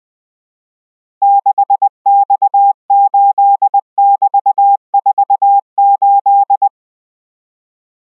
Morse code, heard as a keyed tone, about 20 wpm, 800 Hz. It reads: 6X8<BT>48